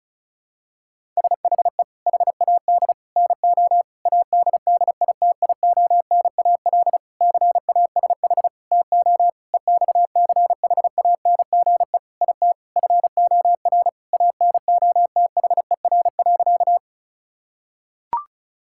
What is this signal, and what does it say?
Morse code, heard as a keyed tone, about 34 wpm, 700 Hz. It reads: SHE HAD NO ADDITIONAL CASH TO EXCHANGE IT FOR ANOTHER.
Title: Sentences from Top 1000 Words - Review Part 6 - 35wpm